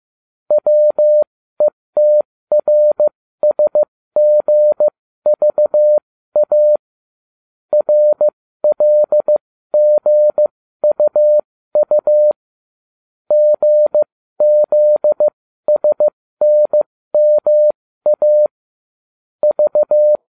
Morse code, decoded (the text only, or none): WETRSGVA RLGUU GZSNMA V